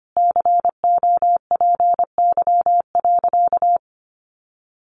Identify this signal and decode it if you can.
Morse code, heard as a keyed tone, about 25 wpm, 700 Hz. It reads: COPY.
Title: Sentences from Top 1400 Words 25wpm